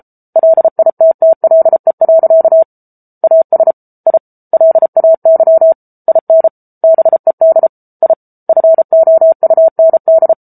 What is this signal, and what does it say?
Morse code, heard as a keyed tone, about 34 wpm, 650 Hz. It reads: LITTLE. AS I LAY IN BED I FOUND